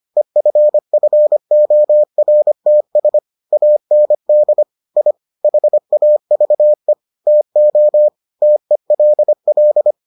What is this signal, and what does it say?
Morse code, heard as a keyed tone, about 25 wpm, 600 Hz. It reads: EFFORTS AND I HAVE TO TELL